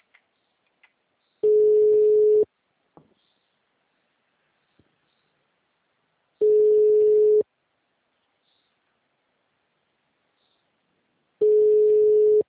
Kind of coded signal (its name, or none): telephone call progress tone